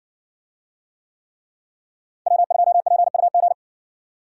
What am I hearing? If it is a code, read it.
Morse code, heard as a keyed tone, about 56 wpm, 700 Hz. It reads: W2PUG